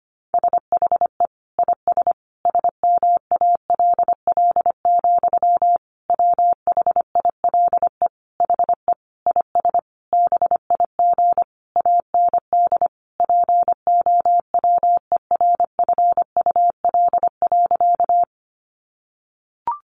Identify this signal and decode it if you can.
Morse code, heard as a keyed tone, about 25 wpm, 700 Hz. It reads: SHE IS SMALL, WHILE HE IS BIG AND POWERFUL.
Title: Sentences from Top 1300 Words 25wpm